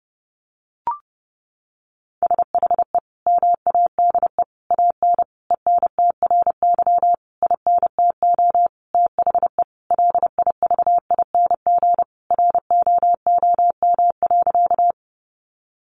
Morse code, heard as a keyed tone, about 30 wpm, 700 Hz. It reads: SHE MADE AN ENTRY INTO THE LIVING ROOM.
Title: Sentences from Top 1500 Words 30wpm